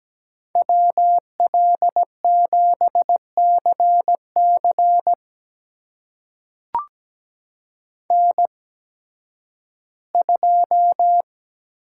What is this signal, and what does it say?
Morse code, heard as a keyed tone, about 17 wpm, spaced as for 11 wpm, 700 Hz. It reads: WL7CC N2
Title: USA Call Signs - 5 Characters - 8x Character Spacing - 17wpm